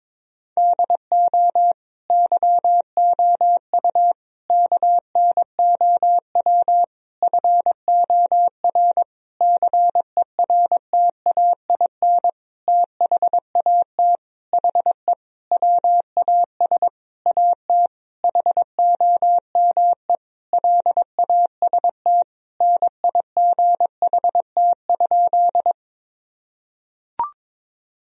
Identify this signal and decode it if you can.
Morse code, heard as a keyed tone, about 22 wpm, 700 Hz. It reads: DO YOU KNOW FOR CERTAIN THAT HE WAS AT HOME LAST NIGHT?
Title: Sentences from Top 1000 Words - Review Part 2 - 22wpm